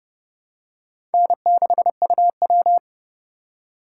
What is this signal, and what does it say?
Morse code, heard as a keyed tone, about 29 wpm, 700 Hz. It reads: N6UW
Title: USA Call Signs - 4 Characters - 4x Character Spacing - 30wpm